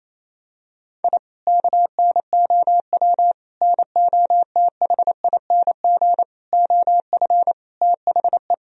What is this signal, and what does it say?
Morse code, heard as a keyed tone, about 28 wpm, 700 Hz. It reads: I KNOW NOTHING OF THE
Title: Sentences from Top 1000 Words - Review Part 3 - 28wpm